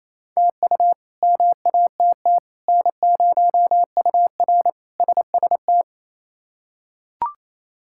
Morse code, heard as a keyed tone, about 27 wpm, 700 Hz. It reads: TU MATT N0UR SST
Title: K1USN SST QSO Practice - 28wpm